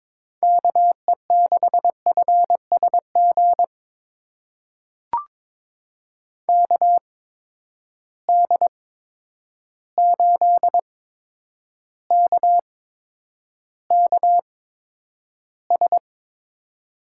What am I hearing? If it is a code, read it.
Morse code, heard as a keyed tone, about 21 wpm, spaced as for 9 wpm, 700 Hz. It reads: KE6FSG KD8KKS
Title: USA Call Signs - 8x Character Spacing - 22wpm